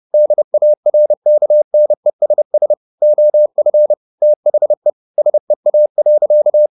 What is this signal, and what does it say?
Morse code, heard as a keyed tone, about 30 wpm, 600 Hz. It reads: DARKNESS OF THE SEA.
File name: the old man and the sea / the old man and the sea 1 30wpm